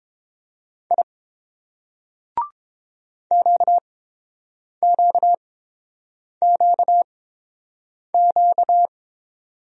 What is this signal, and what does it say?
Morse code, heard as a keyed tone, about 30 wpm, spaced as for 7 wpm, 700 Hz. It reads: I QQQQ